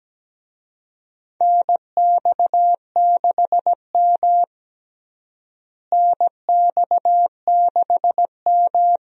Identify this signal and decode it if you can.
Morse code, heard as a keyed tone, about 17 wpm, 700 Hz. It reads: NX6M NX6M
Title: USA Call Signs - 4 Characters - Repeated 2x - 17wpm